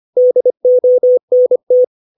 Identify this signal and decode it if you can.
Morse code, heard as a keyed tone, about 25 wpm, 500 Hz. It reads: DONT
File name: 632 words / 632 Words 25WPM